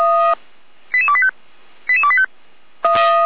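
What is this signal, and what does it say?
Five-tone selective call (ZVEI), heard as quick runs of five tones, tones of about 70 ms. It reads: 88285 88285
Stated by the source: Received on an SDR, from the Feuerwehr Orlamünde audio archive.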